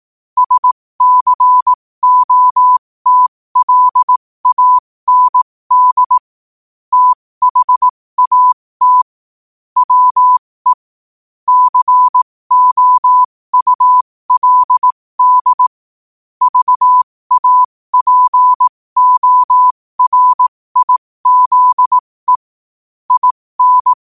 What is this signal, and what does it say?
Morse code, heard as a keyed tone, about 18 wpm, spaced as for 15 wpm, 1000 Hz. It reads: SCOTLAND THAT WE COULD VAPORIZE IN
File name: qotdfiles / 15wpm